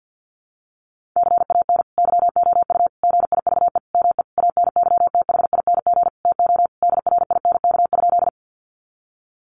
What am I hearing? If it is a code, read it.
Morse code, heard as a keyed tone, about 50 wpm, 700 Hz. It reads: CAN YOU GIVE ME ANYTHING TO DRINK?
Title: Sentences from Top 1500 Words 50wpm